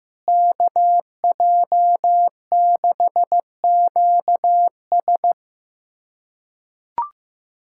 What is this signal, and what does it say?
Morse code, heard as a keyed tone, about 15 wpm, 700 Hz. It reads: KJ6QS
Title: USA Callsign DE Callsign - 5 Characters - 15wpm